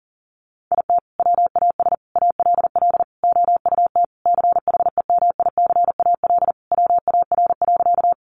Morse code, heard as a keyed tone, about 40 wpm, 700 Hz. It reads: IT WAS ALL OUT CHEMICAL WAR.
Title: Sentences from Top 1200 Words 40wpm